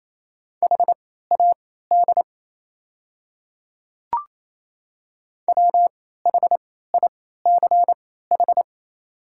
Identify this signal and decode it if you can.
Morse code, heard as a keyed tone, about 28 wpm, spaced as for 16 wpm, 700 Hz. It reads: HAD WHICH
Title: 3x Character Spacing - Top 100 Words - 28wpm